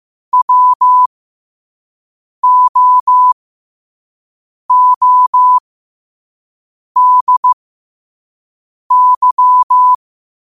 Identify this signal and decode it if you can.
Morse code, heard as a keyed tone, about 15 wpm, spaced as for 5 wpm, 1000 Hz. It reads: WOODY